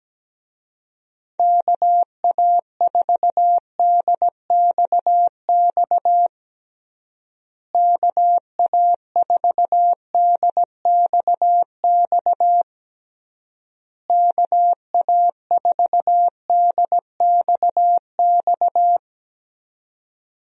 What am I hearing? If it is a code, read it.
Morse code, heard as a keyed tone, about 17 wpm, 700 Hz. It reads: KA4DXX KA4DXX KA4DXX
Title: USA Call Signs - 6 Characters - Repeated 3x - 17wpm